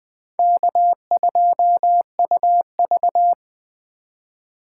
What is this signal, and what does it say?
Morse code, heard as a keyed tone, about 19 wpm, 700 Hz. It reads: K2UV